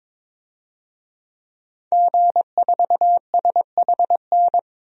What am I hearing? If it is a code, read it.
Morse code, heard as a keyed tone, about 22 wpm, 700 Hz. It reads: G4SHN
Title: DXCC-100 Call Signs - 5 Characters - Repeated 2x - 22wpm